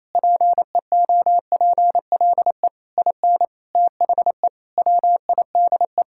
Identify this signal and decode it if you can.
Morse code, heard as a keyed tone, about 27 wpm, 700 Hz. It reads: PEOPLE IN THE WIDE
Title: Sentences from Top 1000 Words - Review Part 6 - 28wpm